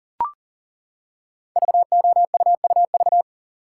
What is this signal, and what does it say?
Morse code, heard as a keyed tone, about 39 wpm, 700 Hz. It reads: VOUUV
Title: Code Groups - 5 Letters - 40wpm